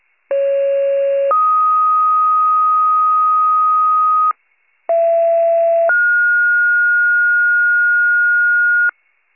Two-tone page, heard as a one-second tone, then a three-second tone, about 550 then 1250 Hz, about 670 then 1430 Hz, with noise. It is heard from 0.3 to 8.9 s.